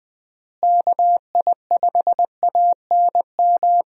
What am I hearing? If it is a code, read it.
Morse code, heard as a keyed tone, about 20 wpm, 700 Hz. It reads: KI5ANM
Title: USA Call Signs - 6 Characters - Repeated 3x - 20wpm